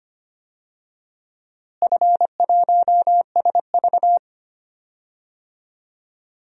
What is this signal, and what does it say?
Morse code, heard as a keyed tone, about 25 wpm, 700 Hz. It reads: F1SV
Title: DXCC-100 Call Signs - 4 Characters - Repeated 3x - 25wpm